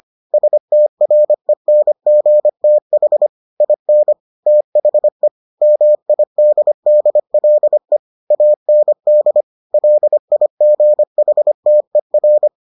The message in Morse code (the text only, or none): STRENGTH IN THE MIDDLE AND LIGHTER